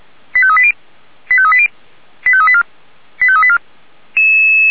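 Five-tone selective call (ZVEI), heard as quick runs of five tones, tones of about 70 ms. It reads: 85380 85380 85384 85384
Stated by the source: Received on an SDR as a ZVEI five-tone call.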